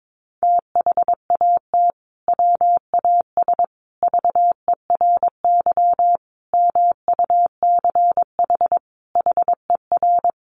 Morse code, heard as a keyed tone, about 22 wpm, 700 Hz. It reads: THAT WAS VERY MUCH HER
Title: Sentences from Top 1200 Words 22wpm